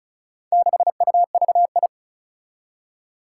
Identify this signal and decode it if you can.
Morse code, heard as a keyed tone, about 36 wpm, 700 Hz. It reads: 6UVI
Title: Four Challenging Letter-Number Combinations 35wpm